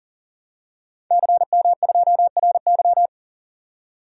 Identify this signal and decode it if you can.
Morse code, heard as a keyed tone, about 40 wpm, 700 Hz. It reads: CM2RY
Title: DXCC-100 Call Signs - 5 Characters - 40wpm